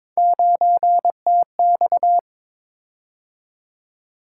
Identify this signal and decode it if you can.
Morse code, heard as a keyed tone, about 22 wpm, 700 Hz. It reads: 9TX
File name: practice / 22wpm / Three Letter-Number Combinations - 22wpm